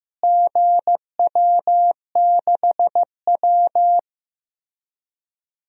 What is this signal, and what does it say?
Morse code, heard as a keyed tone, about 15 wpm, 700 Hz. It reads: GW6W
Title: DXCC-100 Call Signs - 4 Characters - Repeated 2x - 15wpm